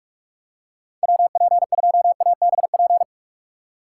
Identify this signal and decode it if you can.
Morse code, heard as a keyed tone, about 46 wpm, 700 Hz. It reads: WP2ABP
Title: USA Call Signs - 6 Characters - 2x Character Spacing - 45wpm